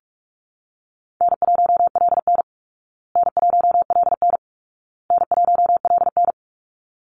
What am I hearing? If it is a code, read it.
Morse code, heard as a keyed tone, about 44 wpm, 700 Hz. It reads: N1LN N1LN N1LN
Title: Easy Call Signs Repeated 3x - 45wpm